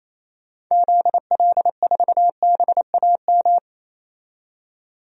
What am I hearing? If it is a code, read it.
Morse code, heard as a keyed tone, about 28 wpm, 700 Hz. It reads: ZL4BAM
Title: DXCC-100 Call Signs - 6 Characters - 28wpm